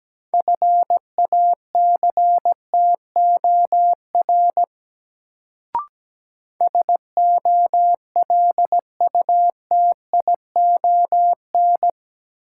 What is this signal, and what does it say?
Morse code, heard as a keyed tone, about 17 wpm, 700 Hz. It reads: FACTOR SOLUTION
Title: Sentences from Top 1300 Words - New Word Drill Alt - 17wpm